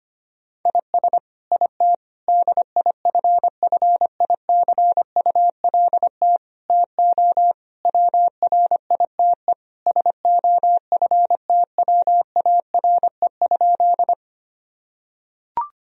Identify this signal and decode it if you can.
Morse code, heard as a keyed tone, about 25 wpm, 700 Hz. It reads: IS IT DIFFICULT TO WRITE SOFTWARE?
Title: Sentences from Top 1000 Words - Review Part 3 - 25wpm